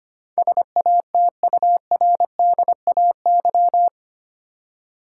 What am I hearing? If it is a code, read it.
Morse code, heard as a keyed tone, about 25 wpm, 700 Hz. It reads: SATURDAY
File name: 8 Letter Words - First 3 Letters Given - 25wpm